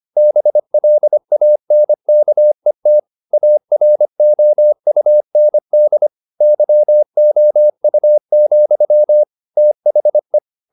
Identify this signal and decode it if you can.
Morse code, heard as a keyed tone, about 25 wpm, 600 Hz. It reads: BLANKET AROUND YOU, THE
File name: the old man and the sea part 1 25wpm